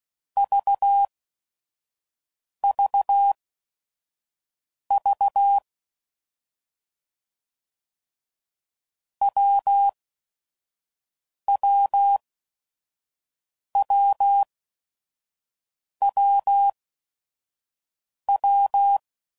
Morse code, heard as a keyed tone, about 16 wpm, spaced as for 5 wpm, 800 Hz. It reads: VVV WWWWW
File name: Combinations / Difficult Combos / AWJ1